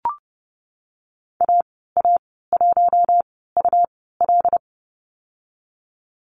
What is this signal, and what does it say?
Morse code, heard as a keyed tone, about 30 wpm, spaced as for 17 wpm, 700 Hz. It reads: AA1UL
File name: Lesson 19g - Letter w - Call Signs 30wpm